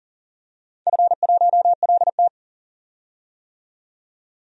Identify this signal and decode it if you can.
Morse code, heard as a keyed tone, about 42 wpm, 700 Hz. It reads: F1LT
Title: DXCC-100 Call Signs - 4 Characters - 40wpm